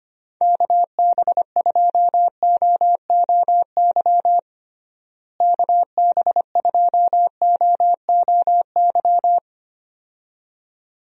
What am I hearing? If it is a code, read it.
Morse code, heard as a keyed tone, about 25 wpm, 700 Hz. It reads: KB2OOY KB2OOY
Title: USA Call Signs - 6 Characters - Repeated 2x - 25wpm